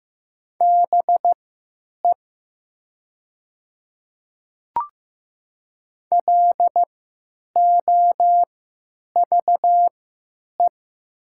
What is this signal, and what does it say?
Morse code, heard as a keyed tone, about 15 wpm, spaced as for 9 wpm, 700 Hz. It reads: BE LOVE
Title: 3x Character Spacing - Top 500 Words - 15wpm